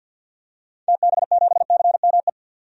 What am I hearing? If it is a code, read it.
Morse code, heard as a keyed tone, about 52 wpm, 700 Hz. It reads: TB7CME
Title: DXCC-100 Call Signs - 6 Characters - 50wpm